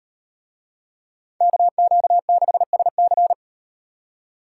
Morse code, heard as a keyed tone, about 37 wpm, 700 Hz. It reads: KQ6SC